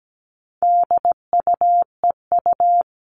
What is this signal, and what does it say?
Morse code, heard as a keyed tone, about 17 wpm, 700 Hz. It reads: DUEU
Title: All Cut Numbers - 4-Digit 17wpm